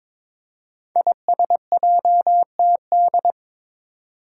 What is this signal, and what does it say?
Morse code, heard as a keyed tone, about 22 wpm, 700 Hz. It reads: ISJTD
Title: Five Challenging Letter-Number Combinations 22wpm